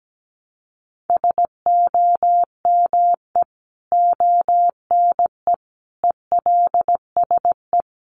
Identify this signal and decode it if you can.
Morse code, heard as a keyed tone, about 17 wpm, 700 Hz. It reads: SOME ONE ELSE
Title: Sentences from Top 1100 Words 17wpm